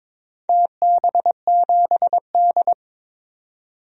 Morse code, heard as a keyed tone, about 22 wpm, 700 Hz. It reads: TB7D
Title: DXCC-100 Call Signs - 4 Characters - 22wpm